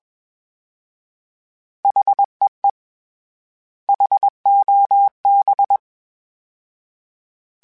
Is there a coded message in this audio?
Yes: Morse code from 1.9 to 5.8 s, heard as a keyed tone, about 21 wpm, 800 Hz. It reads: HEE HOB